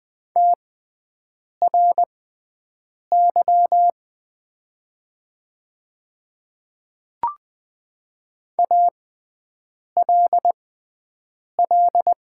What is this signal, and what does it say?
Morse code, heard as a keyed tone, about 20 wpm, spaced as for 7 wpm, 700 Hz. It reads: TRY ALL
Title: 6x Character Spacing - Top 200 Words - 20wpm